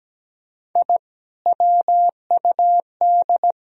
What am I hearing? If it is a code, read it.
Morse code, heard as a keyed tone, about 17 wpm, 700 Hz. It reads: I WUD
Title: Sentences from Top 200 Words and Abbreviations 17wpm